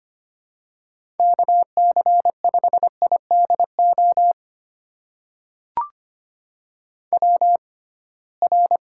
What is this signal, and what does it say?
Morse code, heard as a keyed tone, about 25 wpm, spaced as for 19 wpm, 700 Hz. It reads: KC5IDO WR
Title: USA Call Signs - 6x Character Spacing - 25wpm